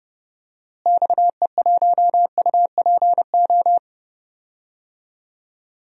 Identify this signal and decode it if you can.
Morse code, heard as a keyed tone, about 30 wpm, 700 Hz. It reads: XE1UPO
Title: DXCC-100 Call Signs - 6 Characters - Repeated 3x - 30wpm